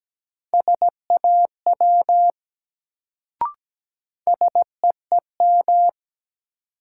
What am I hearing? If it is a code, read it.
Morse code, heard as a keyed tone, about 17 wpm, 700 Hz. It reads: SAW SEEM
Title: Sentences from Top 300 Words - New Word Drill - 17wpm